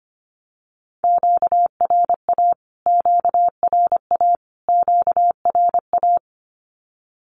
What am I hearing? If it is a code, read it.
Morse code, heard as a keyed tone, about 25 wpm, 700 Hz. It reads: QRA QRA QRA
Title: Q-Codes Repeated 3x 25wpm